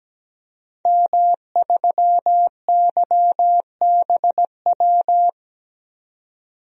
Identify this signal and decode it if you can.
Morse code, heard as a keyed tone, about 17 wpm, 700 Hz. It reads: M3YBW
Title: DXCC-100 Call Signs - 5 Characters - Repeated 3x - 17wpm